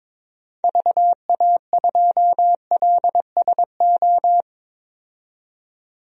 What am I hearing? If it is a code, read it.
Morse code, heard as a keyed tone, about 22 wpm, 700 Hz. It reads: VA2LSO
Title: DXCC-100 Call Signs - 6 Characters - 22wpm